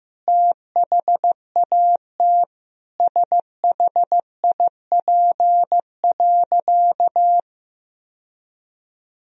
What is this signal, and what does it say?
Morse code, heard as a keyed tone, about 15 wpm, 700 Hz. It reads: THAT SHIP.
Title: Sentences from Top 1000 Words - Review Part 2 - 15wpm